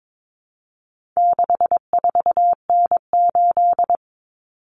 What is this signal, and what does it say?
Morse code, heard as a keyed tone, about 22 wpm, 700 Hz. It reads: 64N8
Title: Common Cut Numbers - 4-Digit 22wpm